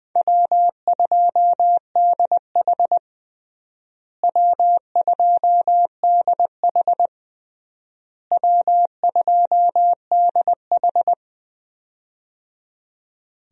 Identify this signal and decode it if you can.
Morse code, heard as a keyed tone, about 20 wpm, 700 Hz. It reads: W2DH W2DH W2DH
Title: USA Call Signs - 4 Characters - Repeated 3x - 20wpm